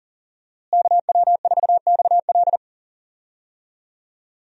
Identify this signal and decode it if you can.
Morse code, heard as a keyed tone, about 40 wpm, 700 Hz. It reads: KW4XL
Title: USA Call Signs - 5 Characters - Repeated 3x - 40wpm